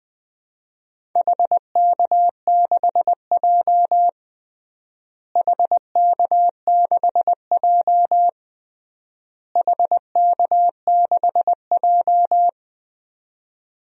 Morse code, heard as a keyed tone, about 20 wpm, 700 Hz. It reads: HK6J HK6J HK6J
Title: DXCC-100 Call Signs - 4 Characters - Repeated 3x - 20wpm